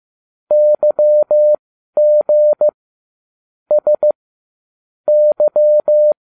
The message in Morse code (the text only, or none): YG S Y